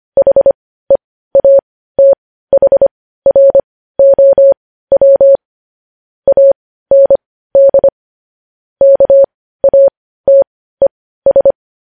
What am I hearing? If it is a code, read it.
Morse code, heard as a keyed tone, about 25 wpm, spaced as for 15 wpm, 550 Hz. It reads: HEATHROW AND KATES
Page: news headlines in morse code at 15 wpm